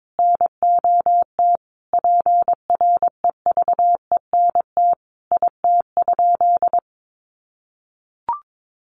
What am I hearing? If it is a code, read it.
Morse code, heard as a keyed tone, about 22 wpm, 700 Hz. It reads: NOT PREVENT IT?
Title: Sentences from Top 1100 Words 22wpm